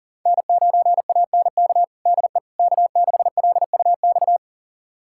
Morse code, heard as a keyed tone, about 40 wpm, 700 Hz. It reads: N9ANK DE K6LUX
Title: USA Callsign DE Callsign 40wpm